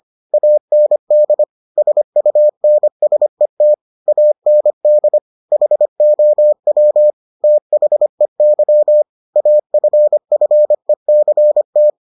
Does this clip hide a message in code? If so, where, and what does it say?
Yes: Morse code from 0.3 to 11.9 s, heard as a keyed tone, about 25 wpm, 600 Hz. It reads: AND SUNSET AND HOW THEY AFFECT